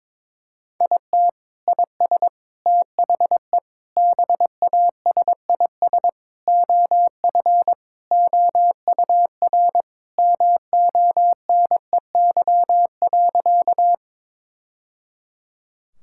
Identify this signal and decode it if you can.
Morse code, heard as a keyed tone, about 22 wpm, 700 Hz. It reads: IT IS THE BASIS OF OUR MONEY.
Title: Sentences from Top 1000 Words - Review Part 5 - 22wpm